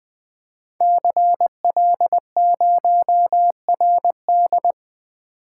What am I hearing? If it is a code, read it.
Morse code, heard as a keyed tone, about 20 wpm, 700 Hz. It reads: CL0RD
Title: Code Groups - 5 Letter:Numbers - 20wpm